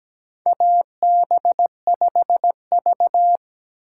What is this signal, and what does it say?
Morse code, heard as a keyed tone, about 17 wpm, 700 Hz. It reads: AB5V
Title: USA Call Signs - 4 Characters - 2x Character Spacing - 17wpm